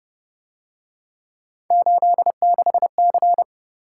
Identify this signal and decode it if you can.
Morse code, heard as a keyed tone, about 30 wpm, 700 Hz. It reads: 86C
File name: Speed-Racing - Three Letter-Number Combinations - 25wpm